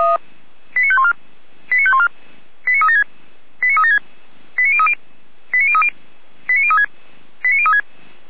Five-tone selective call (ZVEI), heard as quick runs of five tones, tones of about 70 ms. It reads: 89514 89514 89376 89376 89030 89030 89036 89036
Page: SDR/ZVEI